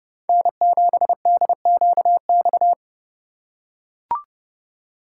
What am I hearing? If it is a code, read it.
Morse code, heard as a keyed tone, about 27 wpm, 700 Hz. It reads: N7DQX